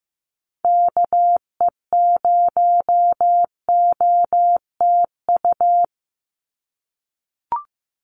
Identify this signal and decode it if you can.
Morse code, heard as a keyed tone, about 15 wpm, 700 Hz. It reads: KE0OTU
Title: USA Call Signs 15wpm